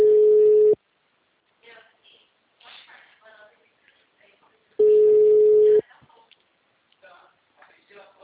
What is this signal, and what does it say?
Ringback tone (Europe, 425 Hz), with speech.
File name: out- -2038- - - .53086